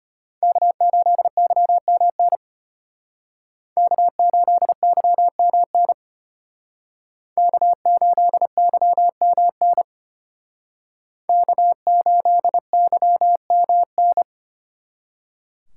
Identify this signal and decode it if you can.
Morse code, heard as a keyed tone, about 32 wpm, 700 Hz. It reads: K8YMN K8YMN K8YMN K8YMN